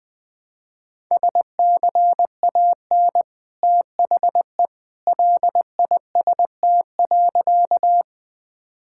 Morse code, heard as a keyed tone, about 20 wpm, 700 Hz. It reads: SCAN THE LIST.
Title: Easy Sentences with Hints - 20wpm